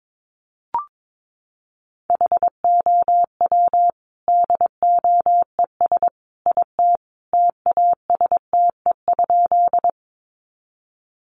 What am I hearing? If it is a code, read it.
Morse code, heard as a keyed tone, about 22 wpm, 700 Hz. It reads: HOW DOES IT TASTE?
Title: Sentences from Top 1100 Words 22wpm